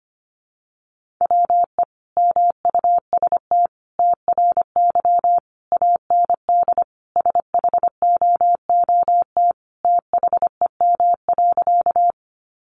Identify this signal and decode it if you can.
Morse code, heard as a keyed tone, about 25 wpm, 700 Hz. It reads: WE MUST TRY AND SHOOT THEM.
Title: Sentences from Top 1200 Words 25wpm